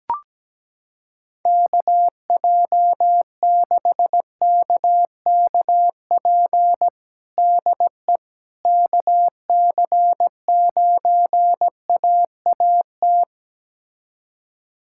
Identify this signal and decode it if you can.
Morse code, heard as a keyed tone, about 17 wpm, 700 Hz. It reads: KJ6KKP DE KC9AAT
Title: USA Callsign DE Callsign - 6 Characters - 17wpm